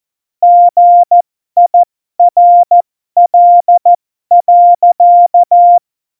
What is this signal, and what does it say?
Morse code, heard as a keyed tone, about 14 wpm, spaced as for 12 wpm, 700 Hz. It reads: GIRL.
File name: Morsekurs / Witze